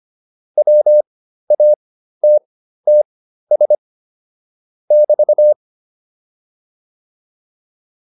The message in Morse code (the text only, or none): WATTS <BT>